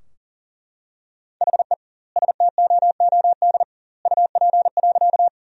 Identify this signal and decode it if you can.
Morse code, heard as a keyed tone, about 40 wpm, 700 Hz. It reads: HE STOOD UP.
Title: 2x Word-Spacing - Sentences from Top 500 Words - 40wpm